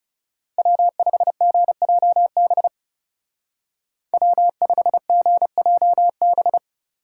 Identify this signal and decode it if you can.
Morse code, heard as a keyed tone, about 36 wpm, 700 Hz. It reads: W5GJB W5GJB
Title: Speed Racing - USA 1x3 Call Signs 30wpm